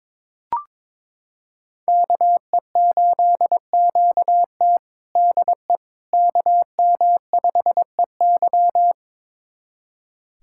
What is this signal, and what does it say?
Morse code, heard as a keyed tone, about 22 wpm, 700 Hz. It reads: KE8QT DE KM5EY
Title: USA Callsign DE Callsign - 5 Characters - 22wpm